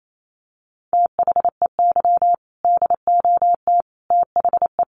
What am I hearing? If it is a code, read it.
Morse code, heard as a keyed tone, about 28 wpm, 700 Hz. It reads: THEY DOT THE
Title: Sentences from Top 1500 Words 28wpm